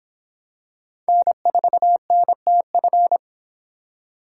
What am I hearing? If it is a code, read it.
Morse code, heard as a keyed tone, about 26 wpm, 700 Hz. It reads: N4NTF